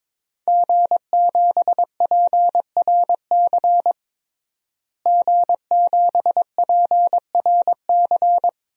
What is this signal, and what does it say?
Morse code, heard as a keyed tone, about 22 wpm, 700 Hz. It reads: G7PRC G7PRC
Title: DXCC-100 Call Signs - 5 Characters - Repeated 2x - 22wpm